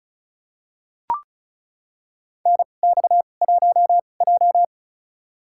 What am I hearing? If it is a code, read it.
Morse code, heard as a keyed tone, about 36 wpm, spaced as for 25 wpm, 700 Hz. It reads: NX1J
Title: USA Call Signs - 2x Character Spacing - 35wpm